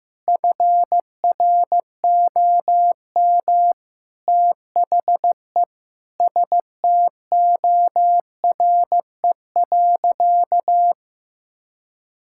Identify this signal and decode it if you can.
Morse code, heard as a keyed tone, about 15 wpm, 700 Hz. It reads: FROM THE STORE.
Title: Sentences from Top 1000 Words - Review Part 5 - 15wpm